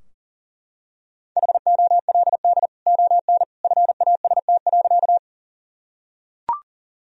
Morse code, heard as a keyed tone, about 40 wpm, 700 Hz. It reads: HOLD ON FAST.
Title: Sentences from Top 1000 Words - Review Part 2 - 40wpm